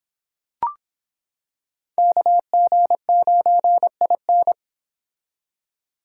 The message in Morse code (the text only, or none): KG9IN